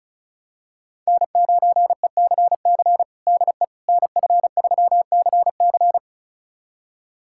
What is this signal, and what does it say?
Morse code, heard as a keyed tone, about 36 wpm, 700 Hz. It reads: N9ECC DE NF3CC
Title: USA Callsign DE Callsign 35wpm